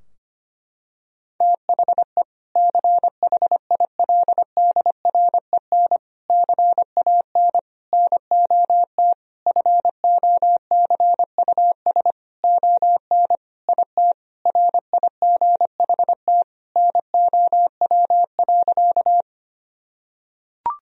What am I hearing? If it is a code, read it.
Morse code, heard as a keyed tone, about 25 wpm, 700 Hz. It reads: THE CHILDREN CAN NOT FOCUS ON IT RIGHT NOW.
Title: Sentences from Top 1000 Words - Review Part 4 - 25wpm